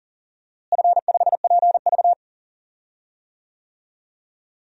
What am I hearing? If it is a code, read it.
Morse code, heard as a keyed tone, about 40 wpm, 700 Hz. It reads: F5PV
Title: DXCC-100 Call Signs - 4 Characters - Repeated 2x - 40wpm